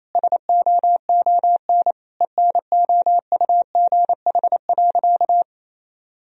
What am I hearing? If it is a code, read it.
Morse code, heard as a keyed tone, about 28 wpm, 700 Hz. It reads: SOON ENOUGH.
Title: Sentences from Top 1000 Words - Review Part 3 - 28wpm